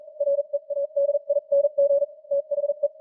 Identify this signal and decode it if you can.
Morse code, heard as a keyed tone, about 41 wpm, spaced as for 35 wpm, 570 Hz. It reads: READING THE